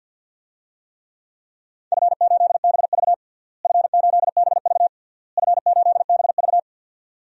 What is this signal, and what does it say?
Morse code, heard as a keyed tone, about 46 wpm, 700 Hz. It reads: F8BV F8BV F8BV
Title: DXCC-100 Call Signs - 4 Characters - Repeated 3x - 50wpm